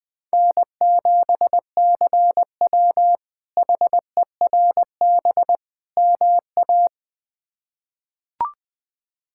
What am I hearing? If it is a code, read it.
Morse code, heard as a keyed tone, about 20 wpm, 700 Hz. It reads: N7CW HERB MA
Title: K1USN SST QSO Practice - 20wpm